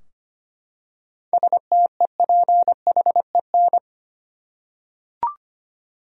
Morse code, heard as a keyed tone, about 25 wpm, 700 Hz. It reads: STEPHEN
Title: USA Common Names - 25wpm